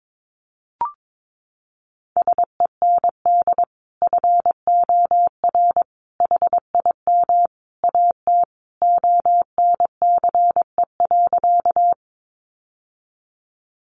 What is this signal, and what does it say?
Morse code, heard as a keyed tone, about 22 wpm, 700 Hz. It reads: SEND FOR HIM AT ONCE.
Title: Sentences from Top 1500 Words 22wpm